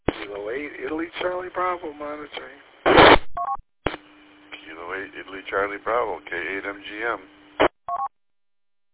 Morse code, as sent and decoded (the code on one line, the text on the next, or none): none